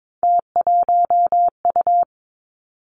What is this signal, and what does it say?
Morse code, heard as a keyed tone, about 22 wpm, 700 Hz. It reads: T1U
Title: Three Challenging Letter-Number Combinations 22wpm